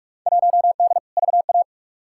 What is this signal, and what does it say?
Morse code, heard as a keyed tone, about 46 wpm, 700 Hz. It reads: 1D VA
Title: Field Day Exchanges - 45wpm